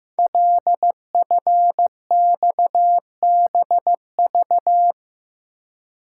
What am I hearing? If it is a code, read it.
Morse code, heard as a keyed tone, about 15 wpm, 700 Hz. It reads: LFXBV
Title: Code Groups - 5 Letters - 15wpm